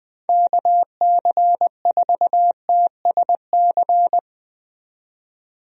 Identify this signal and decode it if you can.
Morse code, heard as a keyed tone, about 20 wpm, 700 Hz. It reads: KC4TSC
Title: USA Call Signs - 6 Characters - 20wpm